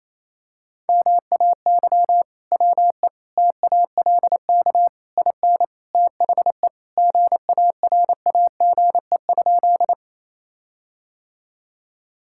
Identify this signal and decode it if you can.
Morse code, heard as a keyed tone, about 28 wpm, 700 Hz. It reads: MAY WE TALK IN THE GARAGE?
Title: Sentences from Top 1600 Words 28wpm